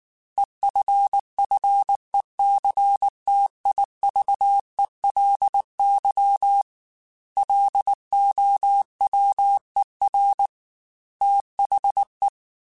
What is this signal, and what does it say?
Morse code, heard as a keyed tone, about 19 wpm, 800 Hz. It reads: EFFECTIVELY LOWER THE
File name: Technical Subjects / Technical Subjects-mp3s / XC @18